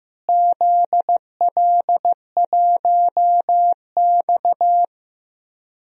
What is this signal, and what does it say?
Morse code, heard as a keyed tone, about 15 wpm, 700 Hz. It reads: ZL1X
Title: DXCC-100 Call Signs - 4 Characters - Repeated 3x - 15wpm